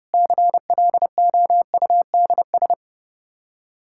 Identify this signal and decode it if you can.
Morse code, heard as a keyed tone, about 32 wpm, 700 Hz. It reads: CLOUDS
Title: Practice QSO - Exchange 2 - Info - 30wpm